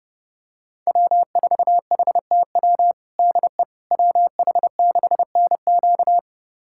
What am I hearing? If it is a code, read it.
Morse code, heard as a keyed tone, about 30 wpm, 700 Hz. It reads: W4HTW DE WH6NQ
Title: USA Callsign DE Callsign - 5 Characters - 30wpm